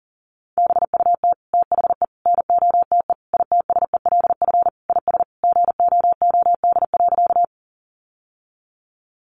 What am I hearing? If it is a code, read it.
Morse code, heard as a keyed tone, about 40 wpm, 700 Hz. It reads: BUT THE NOTE ITSELF IS GOOD.
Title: Sentences from Top 800 Words 40wpm